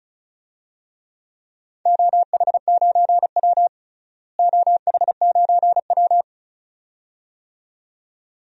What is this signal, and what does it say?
Morse code, heard as a keyed tone, about 36 wpm, 700 Hz. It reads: OH9W OH9W
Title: DXCC-100 Call Signs - 4 Characters - Repeated 2x - 35wpm